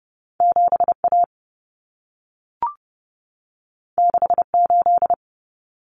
Morse code, heard as a keyed tone, about 30 wpm, 700 Hz. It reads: 7A 68
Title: Common Cut Numbers - 2-Digit 30wpm